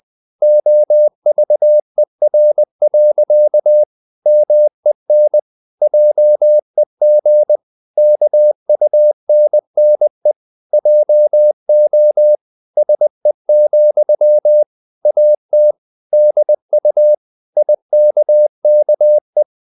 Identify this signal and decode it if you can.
Morse code, heard as a keyed tone, about 20 wpm, 600 Hz. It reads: OVER. MEN JEG KUNNE JO SE, AT DU IKKE
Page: Ferie 20 wpm